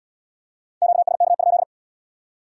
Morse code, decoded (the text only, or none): QSK?